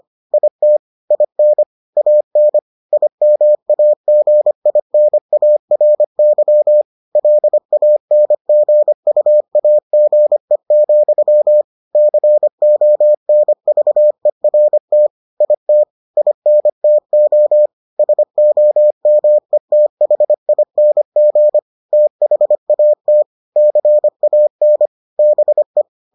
Morse code, heard as a keyed tone, about 25 wpm, 600 Hz. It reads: IT IN AN IMAGINARY LANGUAGE, CONVERT IT INTO SOMETHING THAT CAN BE